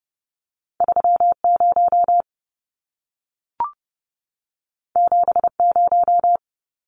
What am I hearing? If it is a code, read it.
Morse code, heard as a keyed tone, about 30 wpm, 700 Hz. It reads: 30 70
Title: Numbers - 2-Digit 30wpm